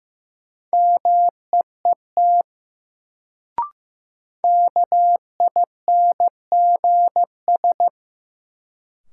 Morse code, heard as a keyed tone, about 15 wpm, 700 Hz. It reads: MEET KINGS
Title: Sentences from Top 1000 Words - Review Part 6 - New Word Drill - 15wpm